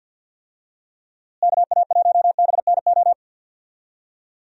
Morse code, heard as a keyed tone, about 52 wpm, 700 Hz. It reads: KA1BNO
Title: USA Call Signs - 6 Characters - Repeated 3x - 50wpm